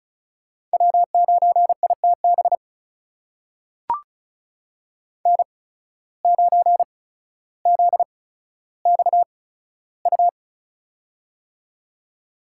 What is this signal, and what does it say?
Morse code, heard as a keyed tone, about 36 wpm, spaced as for 15 wpm, 700 Hz. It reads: W9ITB N9ZXU
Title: USA Call Signs - 8x Character Spacing - 35wpm